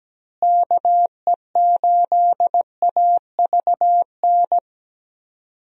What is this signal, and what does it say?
Morse code, heard as a keyed tone, about 17 wpm, 700 Hz. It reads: KE8AVN
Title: USA Call Signs - 6 Characters - Repeated 3x - 17wpm